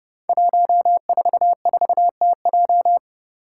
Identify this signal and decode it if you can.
Morse code, heard as a keyed tone, about 29 wpm, 700 Hz. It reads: 144TJ
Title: Five Challenging Letter-Number Combinations 30wpm